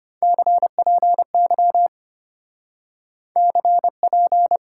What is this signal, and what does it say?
Morse code, heard as a keyed tone, about 30 wpm, 700 Hz. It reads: CPY CP
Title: Speed-Racing Top 90 QSO Elements 25wpm